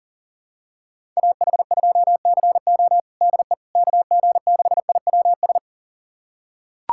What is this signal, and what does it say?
Morse code, heard as a keyed tone, about 40 wpm, 700 Hz. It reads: AH2CO DE KG6IWS
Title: USA Callsign DE Callsign 40wpm